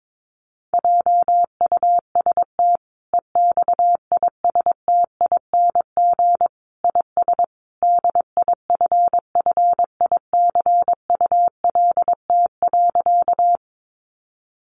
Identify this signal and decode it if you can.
Morse code, heard as a keyed tone, about 22 wpm, 700 Hz. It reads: JUST EXISTING IS DIFFICULT.
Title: Sentences from Top 1500 Words 22wpm